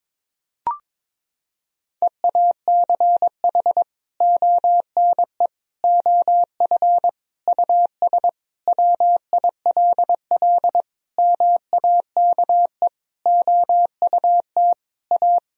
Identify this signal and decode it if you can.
Morse code, heard as a keyed tone, about 22 wpm, 700 Hz. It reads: EACH ONE OF US WILL MAKE OUT A
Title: Sentences from Top 1000 Words - Review Part 2 - 22wpm